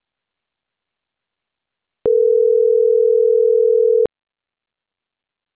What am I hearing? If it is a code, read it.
Ringback tone (North America), 440 + 480 Hz.